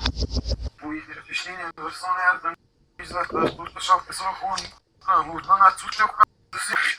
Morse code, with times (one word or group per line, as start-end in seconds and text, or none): none